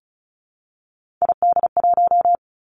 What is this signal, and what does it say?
Morse code, heard as a keyed tone, about 36 wpm, 700 Hz. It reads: ID1
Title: Three Challenging Letter-Number Combinations 35wpm